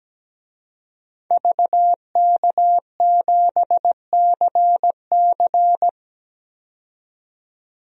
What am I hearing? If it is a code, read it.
Morse code, heard as a keyed tone, about 17 wpm, 700 Hz. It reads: VK7CC
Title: DXCC-100 Call Signs - 5 Characters - 17wpm